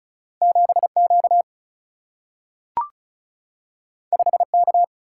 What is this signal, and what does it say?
Morse code, heard as a keyed tone, about 36 wpm, 700 Hz. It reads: 7Q 5K
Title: Single-Letter Single-Number 35wpm